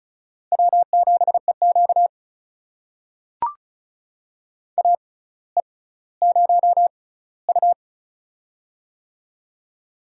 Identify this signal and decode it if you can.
Morse code, heard as a keyed tone, about 36 wpm, spaced as for 18 wpm, 700 Hz. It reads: W7EQ AE0U
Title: USA Call Signs - 4 Characters - 6x Character Spacing - 35wpm